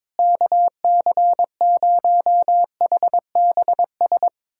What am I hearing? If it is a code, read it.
Morse code, heard as a keyed tone, about 22 wpm, 700 Hz. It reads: KC0HBS
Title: USA Callsign DE Callsign - 6 Characters - 22wpm